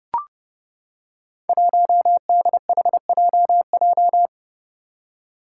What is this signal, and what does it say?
Morse code, heard as a keyed tone, about 31 wpm, 700 Hz. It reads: 1DHJJ
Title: Five Challenging Letter-Number Combinations 30wpm